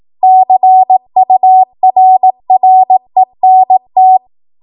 Morse code, heard as a keyed tone, about 18 wpm, 750 Hz. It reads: CURRENT
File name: Morse / 18WPM